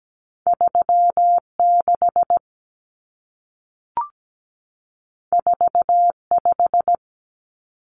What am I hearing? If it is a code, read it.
Morse code, heard as a keyed tone, about 17 wpm, 700 Hz. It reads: 36 45
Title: Numbers - 2-Digit 17wpm